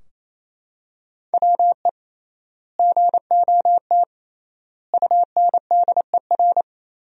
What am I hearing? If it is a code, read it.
Morse code, heard as a keyed tone, about 28 wpm, 700 Hz. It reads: WE GOT UNDER
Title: 3x Word-Spacing - Sentences from Top 300 Words - 28wpm